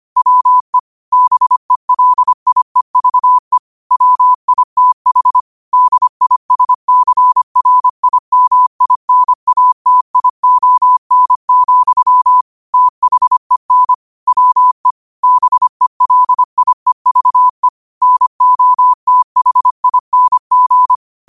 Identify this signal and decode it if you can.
Morse code, heard as a keyed tone, about 25 wpm, 1000 Hz. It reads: WE BELIEVE WITH DISCRIMINATION, THEN WE BELIEVE NOTHING